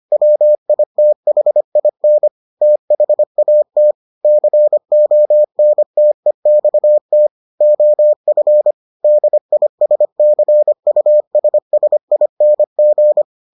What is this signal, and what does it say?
Morse code, heard as a keyed tone, about 25 wpm, 600 Hz. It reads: WITHIN THAT CONTEXT OF DISCUSSING